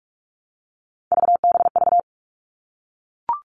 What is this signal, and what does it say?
Morse code, heard as a keyed tone, about 44 wpm, 700 Hz. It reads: VBV
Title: All Cut Numbers - 3-Digit 45wpm